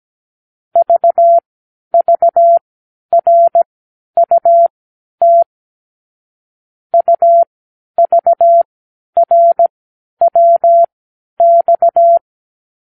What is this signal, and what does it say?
Morse code, heard as a keyed tone, about 17 wpm, spaced as for 10 wpm, 680 Hz. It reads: VVRUT UVRWX
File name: k7qo / Lesson 36 (A-X)